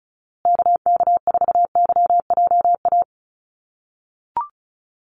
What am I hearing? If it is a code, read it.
Morse code, heard as a keyed tone, about 34 wpm, 700 Hz. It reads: KK4YJA
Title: USA Call Signs Repeated 2x 35wpm